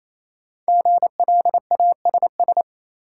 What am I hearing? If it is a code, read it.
Morse code, heard as a keyed tone, about 28 wpm, 700 Hz. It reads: GLASS
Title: Five Letter Words - 28wpm